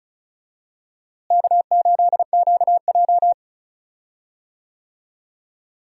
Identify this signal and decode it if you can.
Morse code, heard as a keyed tone, about 36 wpm, 700 Hz. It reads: K8QJ
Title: USA Call Signs - 4 Characters - Repeated 3x - 35wpm